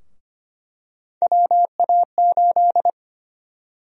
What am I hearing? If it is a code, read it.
Morse code, heard as a keyed tone, about 25 wpm, 700 Hz. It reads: WA8
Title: USA Call Sign Prefixes - 25wpm